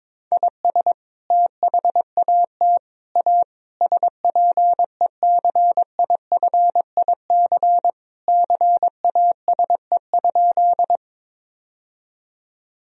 Morse code, heard as a keyed tone, about 22 wpm, 700 Hz. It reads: IS THAT A SPECIFIC CASE?
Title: Sentences from Top 1000 Words - Review Part 3 - 22wpm